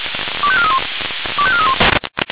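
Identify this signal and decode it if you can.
Five-tone selective call (ZVEI), heard as quick runs of five tones, tones of about 80 ms. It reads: 25431 25431
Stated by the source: ZVEI five-tone selective call received on SDR.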